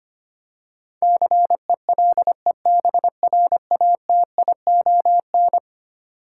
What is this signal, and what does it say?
Morse code, heard as a keyed tone, about 25 wpm, 700 Hz. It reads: CELEBRATION
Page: morse-code-ninja-practice-25wpm